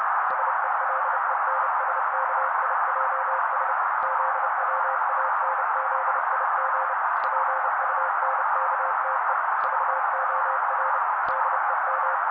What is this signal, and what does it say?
Morse code, heard as a keyed tone, about 30 wpm, 530 Hz. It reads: SELENSKIJS ZWANZIG PUNKTE VOR DEM
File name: daily / 29wpm noise